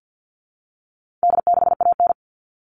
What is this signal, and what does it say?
Morse code, heard as a keyed tone, about 50 wpm, 700 Hz. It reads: D6AN
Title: All Cut Numbers - 4-Digit 50wpm